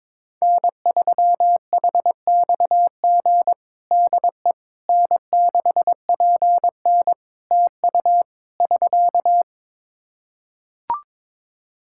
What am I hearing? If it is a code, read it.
Morse code, heard as a keyed tone, about 22 wpm, 700 Hz. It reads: N3HXG DE N6PN TU <SK>
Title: K1USN SST QSO Practice 22wpm